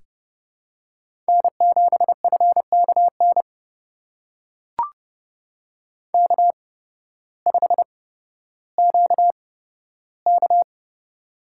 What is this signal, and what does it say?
Morse code, heard as a keyed tone, about 30 wpm, spaced as for 14 wpm, 700 Hz. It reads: N7FKN K5QK
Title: USA Call Signs - 8x Character Spacing - 30wpm